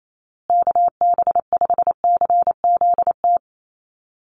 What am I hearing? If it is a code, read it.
Morse code, heard as a keyed tone, about 28 wpm, 700 Hz. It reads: KB5CZT
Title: USA Call Signs Repeated 2x 28wpm